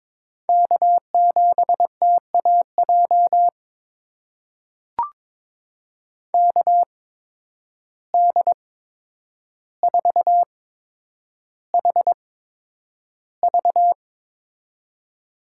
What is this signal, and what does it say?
Morse code, heard as a keyed tone, about 22 wpm, spaced as for 9 wpm, 700 Hz. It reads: K7TAJ KD4HV
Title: USA Call Signs - 5 Characters - 8x Character Spacing - 22wpm